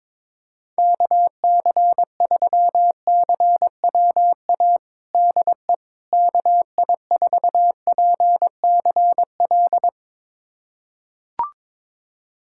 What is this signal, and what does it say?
Morse code, heard as a keyed tone, about 22 wpm, 700 Hz. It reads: KC3CWA DE KI4PCL
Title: USA Callsign DE Callsign 22wpm